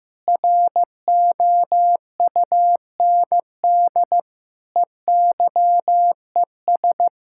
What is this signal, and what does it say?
Morse code, heard as a keyed tone, about 14 wpm, 700 Hz. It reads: ROUND EYES
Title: Sentences from Top 1000 Words - Review Part 6 - 15wpm